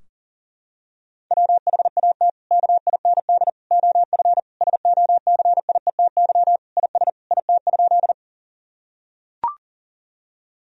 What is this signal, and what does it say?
Morse code, heard as a keyed tone, about 37 wpm, 700 Hz. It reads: WHAT KIND OF SOCIETY IS IT?
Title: Sentences from Top 1000 Words - Review Part 4 - 40wpm